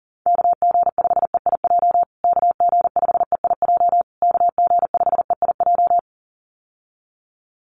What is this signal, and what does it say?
Morse code, heard as a keyed tone, about 40 wpm, 700 Hz. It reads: KG5EIJ KG5EIJ KG5EIJ
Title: USA Call Signs Repeated 3x 40wpm